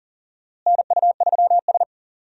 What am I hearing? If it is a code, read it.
Morse code, heard as a keyed tone, about 40 wpm, 700 Hz. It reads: NU3S
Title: USA Call Signs - 4 Characters - Repeated 3x - 40wpm